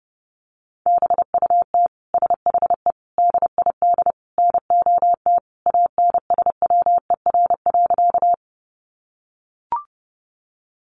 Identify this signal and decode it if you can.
Morse code, heard as a keyed tone, about 30 wpm, 700 Hz. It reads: BUT SHE DID NOT ANSWER.
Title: Sentences from Top 600 Words 30wpm